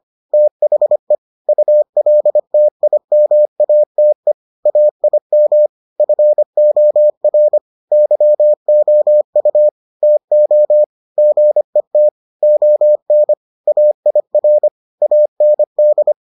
Morse code, heard as a keyed tone, about 25 wpm, 600 Hz. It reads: THE ULTIMATE AIM FOR YOU TO GET ON AIR AND